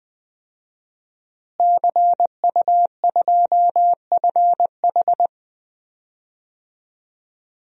Morse code, heard as a keyed tone, about 20 wpm, 700 Hz. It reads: CU2FH
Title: DXCC-100 Call Signs - 5 Characters - Repeated 3x - 20wpm